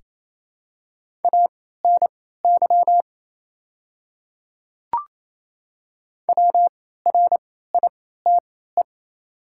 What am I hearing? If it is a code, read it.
Morse code, heard as a keyed tone, about 27 wpm, spaced as for 16 wpm, 700 Hz. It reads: ANY WRITE
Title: 3x Character Spacing - Top 200 Words - 28wpm